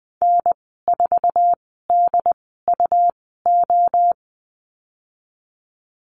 Morse code, heard as a keyed tone, about 20 wpm, spaced as for 15 wpm, 700 Hz. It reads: N4DUO
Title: 147 Lesson 24g - Letter y - Call Signs 20wpm